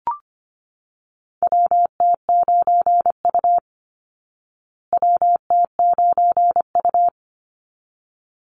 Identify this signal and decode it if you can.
Morse code, heard as a keyed tone, about 25 wpm, 700 Hz. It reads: WT9U WT9U
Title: Call Signs Repeated 3x - 25wpm